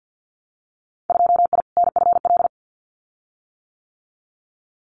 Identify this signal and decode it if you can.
Morse code, heard as a keyed tone, about 50 wpm, 700 Hz. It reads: 2I NFL
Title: Winter Field Day Exchanges 50wpm